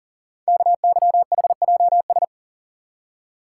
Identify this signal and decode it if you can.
Morse code, heard as a keyed tone, about 40 wpm, 700 Hz. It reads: KYHJS
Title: Code Groups - 5 Letters - 40wpm